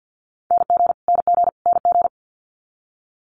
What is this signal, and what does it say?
Morse code, heard as a keyed tone, about 50 wpm, 700 Hz. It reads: NG NG NG
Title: Multi-letter Phonemes Repeated 3x 50wpm